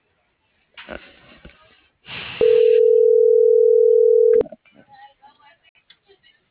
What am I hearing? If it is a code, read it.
Ringback tone (North America), 440 + 480 Hz.